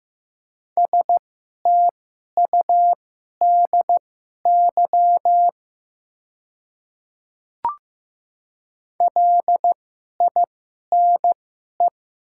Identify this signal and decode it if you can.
Morse code, heard as a keyed tone, about 15 wpm, spaced as for 11 wpm, 700 Hz. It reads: STUDY LINE
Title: 2x Character Spacing - Top 300 Words - 15wpm